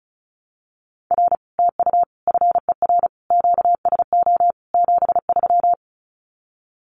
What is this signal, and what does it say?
Morse code, heard as a keyed tone, about 36 wpm, 700 Hz. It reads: R TU FER QSO 73
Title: Common QSO Phrases Encore 35wpm